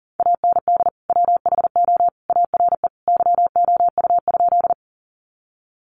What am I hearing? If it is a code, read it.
Morse code, heard as a keyed tone, about 40 wpm, 700 Hz. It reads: AND WHO ARE YOU?
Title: Sentences from Top 100 Words 40wpm